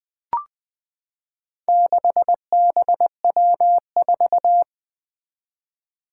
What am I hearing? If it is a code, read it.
Morse code, heard as a keyed tone, about 20 wpm, 700 Hz. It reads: 6BW4
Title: Four Challenging Letter-Number Combinations 20wpm